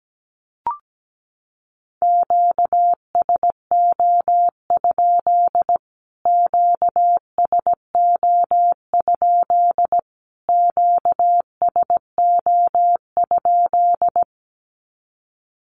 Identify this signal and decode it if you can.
Morse code, heard as a keyed tone, about 17 wpm, 700 Hz. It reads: QSO? QSO? QSO?
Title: Q-Codes Repeated 3x 17wpm